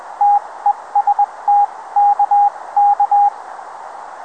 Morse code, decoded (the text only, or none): TEST KK